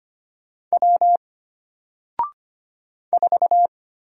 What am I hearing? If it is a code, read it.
Morse code, heard as a keyed tone, about 25 wpm, 700 Hz. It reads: W 4
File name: Single-Letter-Number-25wpm